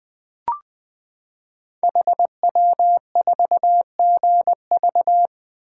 Morse code, heard as a keyed tone, about 20 wpm, 700 Hz. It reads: HW4GV
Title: Code Groups - 5 Letter:Numbers - 20wpm